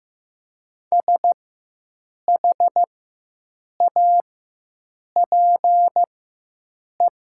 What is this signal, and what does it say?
Morse code, heard as a keyed tone, about 15 wpm, spaced as for 7 wpm, 700 Hz. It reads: SHAPE
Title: 4x Character Spacing - Top 500 Words - 15wpm